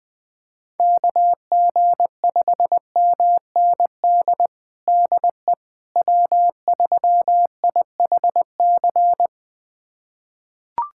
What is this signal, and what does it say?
Morse code, heard as a keyed tone, about 20 wpm, 700 Hz. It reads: KG5MND DE W3IHC
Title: USA Callsign DE Callsign 20wpm